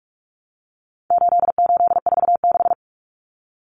Morse code, heard as a keyed tone, about 44 wpm, 700 Hz. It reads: 8846
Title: Numbers - 4-Digit 45wpm